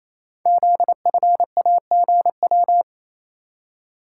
Morse code, heard as a keyed tone, about 29 wpm, 700 Hz. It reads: ZFAGW
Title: Code Groups - 5 Letters - 28wpm